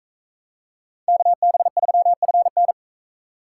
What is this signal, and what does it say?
Morse code, heard as a keyed tone, about 43 wpm, 700 Hz. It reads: KB3FN